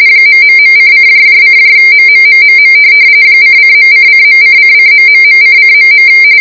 RTTY, about 45.45 baud. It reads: ...:::MMMMM ::.::.:::...::..